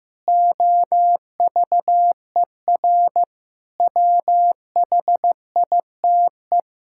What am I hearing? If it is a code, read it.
Morse code, heard as a keyed tone, about 15 wpm, 700 Hz. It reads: OVER WHITE
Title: Pangram Challenge v2 - 15wpm